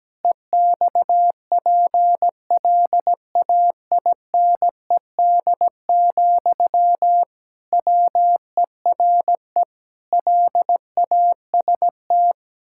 Morse code, heard as a keyed tone, about 17 wpm, 700 Hz. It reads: EXPLAINED, WERE LAST
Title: Pangram Challenge v2 - 17wpm